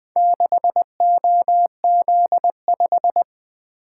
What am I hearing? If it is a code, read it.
Morse code, heard as a keyed tone, about 24 wpm, 700 Hz. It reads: 6OZ5
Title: Speed-Racing - Four Letter-Number Combinations - 20wpm